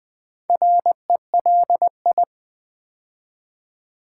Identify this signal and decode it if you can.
Morse code, heard as a keyed tone, about 20 wpm, 700 Hz. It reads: RELI
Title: Build A Word - 2 Letters At A Time - 6-Letter Words - 20wpm